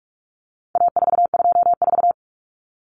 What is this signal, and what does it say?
Morse code, heard as a keyed tone, about 44 wpm, 700 Hz. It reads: A424
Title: Common Cut Numbers - 4-Digit 45wpm